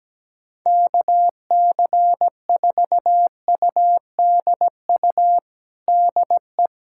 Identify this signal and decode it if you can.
Morse code, heard as a keyed tone, about 17 wpm, 700 Hz. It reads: KC4UDU DE
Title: USA Callsign DE Callsign - 6 Characters - 17wpm